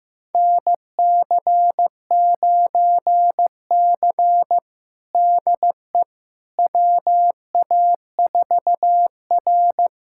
Morse code, heard as a keyed tone, about 15 wpm, 700 Hz. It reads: NC9C DE WA4R
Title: USA Callsign DE Callsign - 4 Characters - 15wpm